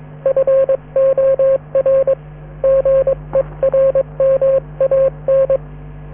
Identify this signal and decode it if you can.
Morse code, heard as a keyed tone, about 21 wpm, 540 Hz, with radio noise. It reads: FOR GERMAN